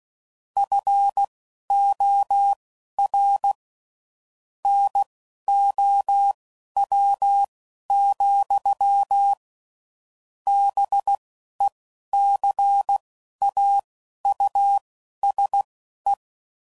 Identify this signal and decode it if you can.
Morse code, heard as a keyed tone, about 16 wpm, spaced as for 11 wpm, 800 Hz. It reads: FOR NOW, BECAUSE
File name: CW Code Groups @10WPM / Technical Subjects / Gates Intro